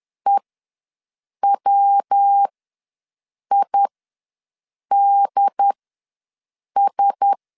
Morse code, heard as a keyed tone, about 10 wpm, spaced as for 6 wpm, 800 Hz. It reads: EWIDS
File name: LEZIONI / 5 LBVC / 5 L ws 10wpm 3636